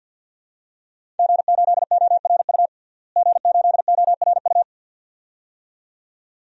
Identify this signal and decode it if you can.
Morse code, heard as a keyed tone, about 47 wpm, 700 Hz. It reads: G8ORU G8ORU
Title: DXCC-100 Call Signs - 5 Characters - Repeated 2x - 50wpm